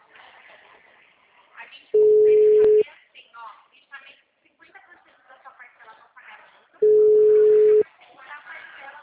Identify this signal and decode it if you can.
Ringback tone (Europe, 425 Hz), with speech.
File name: out- -2011- - - .61110